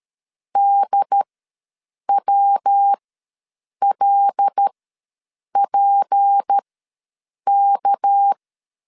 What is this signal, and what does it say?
Morse code, heard as a keyed tone, about 13 wpm, spaced as for 7 wpm, 800 Hz. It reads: DWLPK
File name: LEZIONI / 7 GF9Z / 7 GF9 ws 12wpm 3344